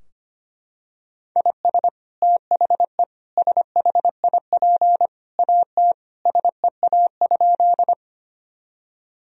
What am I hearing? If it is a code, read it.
Morse code, heard as a keyed tone, about 25 wpm, 700 Hz. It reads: IS THE SHIP AT SEA?
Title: Sentences from Top 1000 Words - Review Part 2 - 25wpm